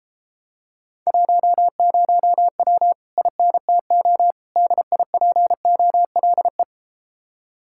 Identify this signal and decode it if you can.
Morse code, heard as a keyed tone, about 33 wpm, 700 Hz. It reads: 10W INTO DIPOLE